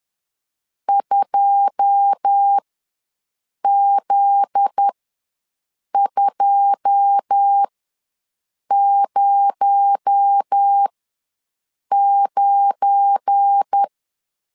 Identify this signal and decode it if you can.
Morse code, heard as a keyed tone, about 10 wpm, spaced as for 6 wpm, 800 Hz. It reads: 2Z209
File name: LEZIONI / 9 2 ws 10wpm 4130